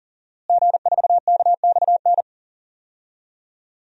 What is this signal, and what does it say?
Morse code, heard as a keyed tone, about 40 wpm, 700 Hz. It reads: G4KXN
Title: DXCC-100 Call Signs - 5 Characters - Repeated 2x - 40wpm